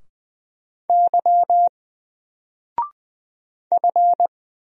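Morse code, heard as a keyed tone, about 20 wpm, 700 Hz. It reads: Y F
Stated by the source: Single letter and number Morse drill at 20 wpm, from N4LCD.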